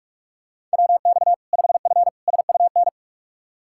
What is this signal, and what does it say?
Morse code, heard as a keyed tone, about 41 wpm, 700 Hz. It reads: WX 5F SUN
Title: Practice QSO - Exchange 2 - Info - 45wpm